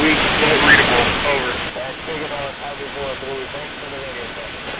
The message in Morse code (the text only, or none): none